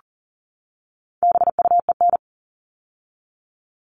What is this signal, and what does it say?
Morse code, heard as a keyed tone, about 40 wpm, 700 Hz. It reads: BUEN
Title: All Cut Numbers - 4-Digit 40wpm